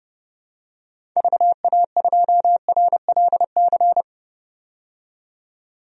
Morse code, heard as a keyed tone, about 30 wpm, 700 Hz. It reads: VA2RLC
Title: DXCC-100 Call Signs - 6 Characters - 30wpm